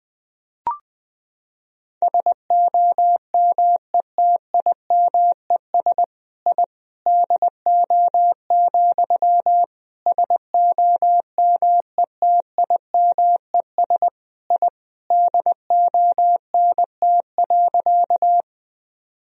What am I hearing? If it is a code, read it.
Morse code, heard as a keyed tone, about 20 wpm, 700 Hz. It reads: SOMETIMES I DO, SOMETIMES I DONT.
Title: Sentences from Top 1000 Words - Review Part 3 - 20wpm